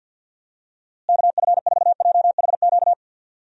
Morse code, heard as a keyed tone, about 53 wpm, 700 Hz. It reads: KF4JHQ
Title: USA Call Signs - 6 Characters - Repeated 3x - 50wpm